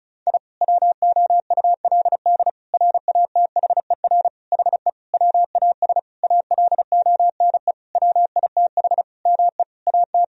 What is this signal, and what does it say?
Morse code, heard as a keyed tone, about 34 wpm, 700 Hz. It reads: I WOULD RATHER HE WAS ALONE WITH ME AT
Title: Sentences from Top 1000 Words - Review Part 4 - 35wpm